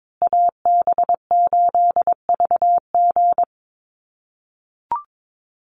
Morse code, heard as a keyed tone, about 22 wpm, 700 Hz. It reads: AB8VG
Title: USA Call Signs Repeated 3x 22wpm